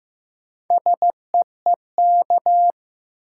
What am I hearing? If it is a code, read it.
Morse code, heard as a keyed tone, about 15 wpm, 700 Hz. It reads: SEEK
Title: Sentences from Top 1600 Words - New Word Drill Alt - 15wpm